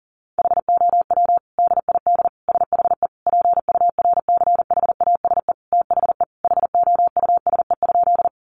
Morse code, heard as a keyed tone, about 40 wpm, 700 Hz. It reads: HOW DID SHE PURCHASE THE HOUSE?
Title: Sentences from Top 800 Words 40wpm